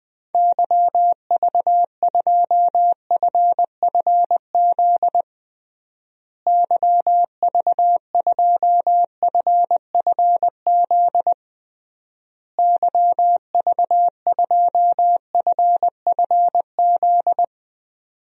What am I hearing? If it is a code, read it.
Morse code, heard as a keyed tone, about 21 wpm, 700 Hz. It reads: YV2FFZ YV2FFZ YV2FFZ
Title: DXCC-100 Call Signs - 6 Characters - Repeated 3x - 20wpm